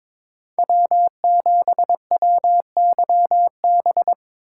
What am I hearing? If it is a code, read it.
Morse code, heard as a keyed tone, about 22 wpm, 700 Hz. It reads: W7WYB
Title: USA Call Signs - 5 Characters - 3x Character Spacing - 22wpm